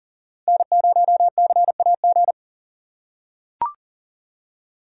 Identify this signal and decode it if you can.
Morse code, heard as a keyed tone, about 40 wpm, 700 Hz. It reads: N0CAG
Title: USA Call Signs - 5 Characters - 2x Character Spacing - 40wpm